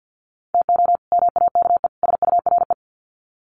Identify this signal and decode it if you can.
Morse code, heard as a keyed tone, about 50 wpm, 700 Hz. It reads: TO MAKE SURE
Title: Sets of 3 Words 50wpm